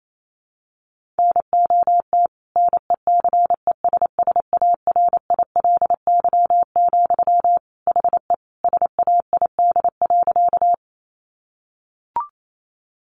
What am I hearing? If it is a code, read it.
Morse code, heard as a keyed tone, about 28 wpm, 700 Hz. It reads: NOT NECESSARILY, HE SAID.
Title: Sentences from Top 1400 Words 28wpm